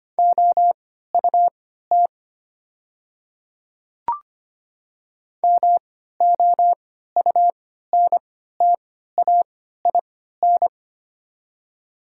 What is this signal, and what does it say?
Morse code, heard as a keyed tone, about 25 wpm, spaced as for 14 wpm, 700 Hz. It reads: OUT MOUNTAIN
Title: 3x Character Spacing - Top 500 Words - 25wpm